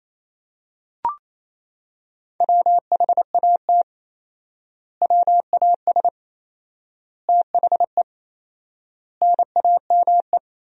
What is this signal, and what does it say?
Morse code, heard as a keyed tone, about 27 wpm, 700 Hz. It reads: WHAT WAS THE NAME
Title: 4x Word-Spacing - Sentences from Top 300 Words - 28wpm